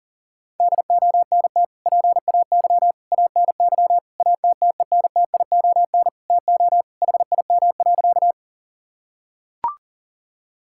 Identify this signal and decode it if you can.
Morse code, heard as a keyed tone, about 40 wpm, 700 Hz. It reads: DONT PAY ANY ATTENTION TO HIM.
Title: Sentences from Top 1000 Words - Review Part 6 - 40wpm